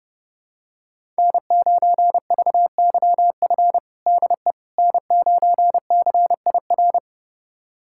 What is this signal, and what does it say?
Morse code, heard as a keyed tone, about 30 wpm, 700 Hz. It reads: N9VYF DE N9CIR
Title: USA Callsign DE Callsign - 5 Characters - 30wpm